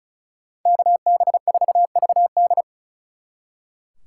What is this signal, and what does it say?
Morse code, heard as a keyed tone, about 43 wpm, 700 Hz. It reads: KB4VD
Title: Speed Racing - USA 2x2 Call Signs 35wpm